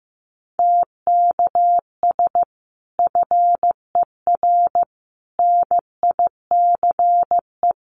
Morse code, heard as a keyed tone, about 15 wpm, 700 Hz. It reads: TKS FER NICE
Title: Common QSO Phrases 15wpm